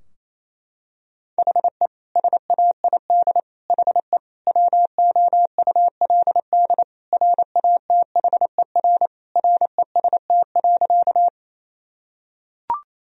Morse code, heard as a keyed tone, about 28 wpm, 700 Hz. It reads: HE SAID HE WOULD RATHER REST.
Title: Sentences from Top 1000 Words - Review Part 4 - 28wpm